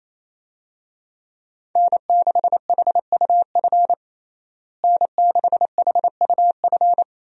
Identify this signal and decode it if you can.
Morse code, heard as a keyed tone, about 28 wpm, 700 Hz. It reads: N6HUF N6HUF
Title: USA Call Signs - 5 Characters - Repeated 2x - 28wpm